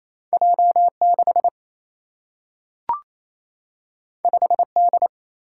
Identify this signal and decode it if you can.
Morse code, heard as a keyed tone, about 28 wpm, 700 Hz. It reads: J 6 5 D
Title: Single-Letter Single-Number 28wpm